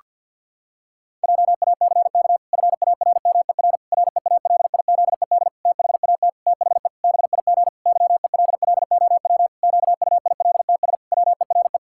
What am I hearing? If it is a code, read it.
Morse code, heard as a keyed tone, about 50 wpm, 700 Hz. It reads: JACK FARMER REALIZED THAT THE BIG YELLOW QUILTS WERE
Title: Pangram Challenge v2 - 50wpm